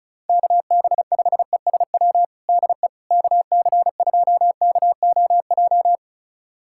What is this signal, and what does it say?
Morse code, heard as a keyed tone, about 36 wpm, 700 Hz. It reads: KB5ESW DE KC2KOJ
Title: USA Callsign DE Callsign 35wpm